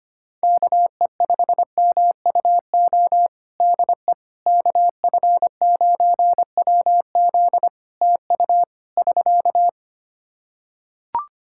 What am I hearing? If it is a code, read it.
Morse code, heard as a keyed tone, about 25 wpm, 700 Hz. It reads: KE5MUO DE KF9WZ TU <SK>
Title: K1USN SST QSO Practice 25wpm